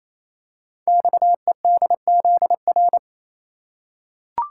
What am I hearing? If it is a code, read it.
Morse code, heard as a keyed tone, about 29 wpm, 700 Hz. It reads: XEDZR
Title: Code Groups - 5 Letters - 28wpm